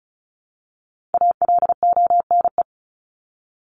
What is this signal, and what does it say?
Morse code, heard as a keyed tone, about 34 wpm, 700 Hz. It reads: ALONE
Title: Top 751-1000 Words 35wpm